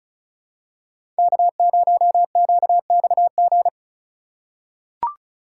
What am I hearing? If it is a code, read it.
Morse code, heard as a keyed tone, about 34 wpm, 700 Hz. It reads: K0QXG
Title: USA Call Signs - 3x Character Spacing - 35wpm